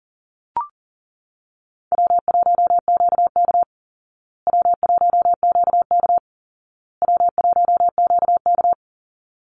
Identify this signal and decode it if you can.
Morse code, heard as a keyed tone, about 40 wpm, 700 Hz. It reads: W1QK W1QK W1QK
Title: Easy Call Signs Repeated 3x - 40wpm